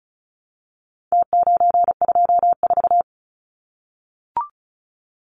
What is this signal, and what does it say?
Morse code, heard as a keyed tone, about 34 wpm, 700 Hz. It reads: T924
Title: Numbers - Contest Style - 4-Digit 35wpm